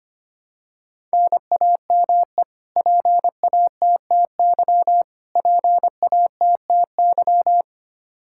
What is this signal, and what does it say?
Morse code, heard as a keyed tone, about 25 wpm, 700 Hz. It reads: NAME PATTY PATTY
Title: Practice QSO - Exchange 1 - Beginning - 25wpm